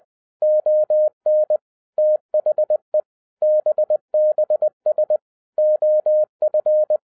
Morse code, heard as a keyed tone, about 20 wpm, 610 Hz. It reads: ON THE BBS OF
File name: 2012-01-mp3 / n16-THROUGH EIGHT MONTHS-20wpm